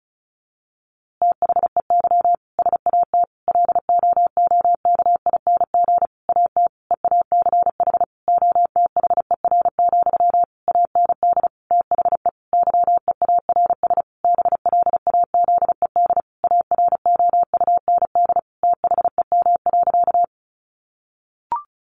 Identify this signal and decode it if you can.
Morse code, heard as a keyed tone, about 34 wpm, 700 Hz. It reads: THEY SAT LOOKING AT EACH OTHER, AND THE YEARS BLAZED AROUND THEM.
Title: A Sound Of Thunder 35wpm